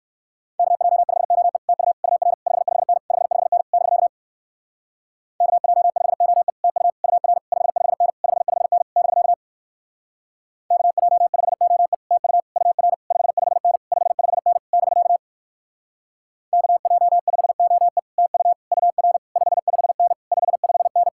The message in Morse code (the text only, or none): KJ5OE TU UR 55N 55N <BK> KJ5OE TU UR 55N 55N <BK> KJ5OE TU UR 55N 55N <BK> KJ5OE TU UR 55N 55N